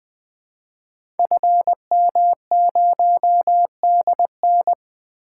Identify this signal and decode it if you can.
Morse code, heard as a keyed tone, about 20 wpm, 700 Hz. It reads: FM0DN
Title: Code Groups - 5 Letter:Numbers - 20wpm